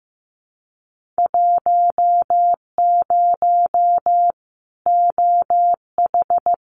Morse code, heard as a keyed tone, about 15 wpm, 700 Hz. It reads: 10 OH
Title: Winter Field Day Exchanges 15wpm